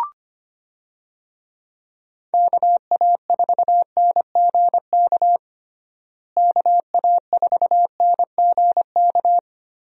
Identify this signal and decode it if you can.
Morse code, heard as a keyed tone, about 25 wpm, 700 Hz. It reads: KA4NGK KA4NGK
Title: USA Call Signs - 6 Characters - Repeated 3x - 25wpm